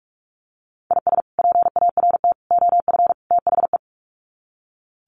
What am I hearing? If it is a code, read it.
Morse code, heard as a keyed tone, about 44 wpm, 700 Hz. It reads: IS PART OF THE
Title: Sets of 4 Words 45wpm